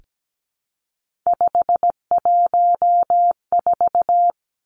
Morse code, heard as a keyed tone, about 17 wpm, 700 Hz. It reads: 514
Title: Numbers - 3-Digit 17wpm